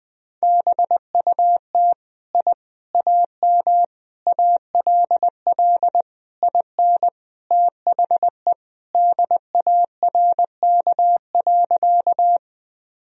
Morse code, heard as a keyed tone, about 20 wpm, 700 Hz. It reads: BUT I AM ALL IN THE DARK.
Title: Sentences from Top 1000 Words - Review Part 2 - 20wpm